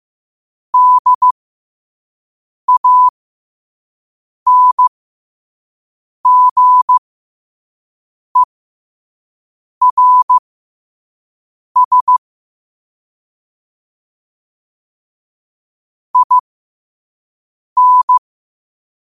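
Morse code, heard as a keyed tone, about 15 wpm, spaced as for 5 wpm, 1000 Hz. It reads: DANGERS IN
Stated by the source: Morse quote of the day at 5 wpm.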